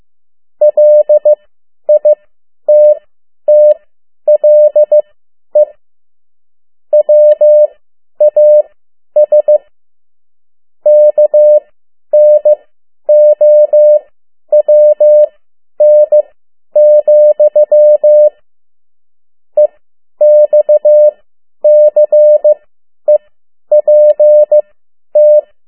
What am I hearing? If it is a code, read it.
Morse code, heard as a keyed tone, about 15 wpm, spaced as for 10 wpm, 600 Hz. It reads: LITTLE WAS KNOWN, EXCEPT
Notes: Morse code 10 WPM